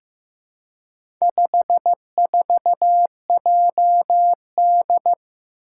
Morse code, heard as a keyed tone, about 15 wpm, 700 Hz. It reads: 54JD
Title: Four Challenging Letter-Number Combinations 15wpm